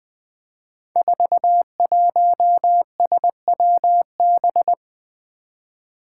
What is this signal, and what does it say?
Morse code, heard as a keyed tone, about 21 wpm, 700 Hz. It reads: 41SWB
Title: Five Challenging Letter-Number Combinations 20wpm